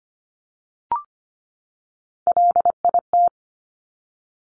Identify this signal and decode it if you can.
Morse code, heard as a keyed tone, about 25 wpm, 700 Hz. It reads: LIT
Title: Top Three Letter Combinations in Words 25wpm